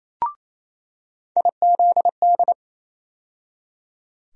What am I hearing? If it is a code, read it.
Morse code, heard as a keyed tone, about 28 wpm, 700 Hz. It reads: IZD
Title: USA Call Sign Postfixes - 3 Letters 28wpm